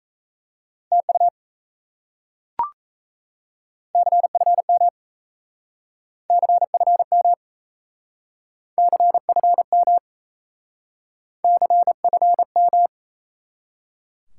Morse code, heard as a keyed tone, about 36 wpm, 700 Hz. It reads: TU CFM CFM CFM CFM